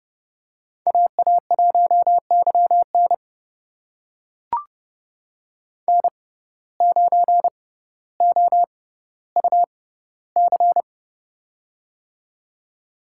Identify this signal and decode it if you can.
Morse code, heard as a keyed tone, about 30 wpm, spaced as for 15 wpm, 700 Hz. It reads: AA1YN N9OUC
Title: USA Call Signs - 5 Characters - 6x Character Spacing - 30wpm